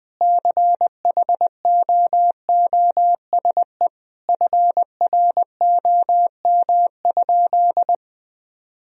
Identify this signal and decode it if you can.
Morse code, heard as a keyed tone, about 20 wpm, 700 Hz. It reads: CHOOSE FROM?
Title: Sentences from Top 1000 Words - Review Part 4 - 20wpm